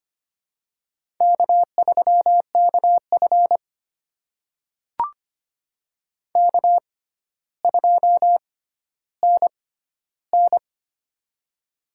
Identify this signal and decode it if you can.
Morse code, heard as a keyed tone, about 25 wpm, spaced as for 13 wpm, 700 Hz. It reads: K3KF K2NN
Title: USA Call Signs - 4 Characters - 6x Character Spacing - 25wpm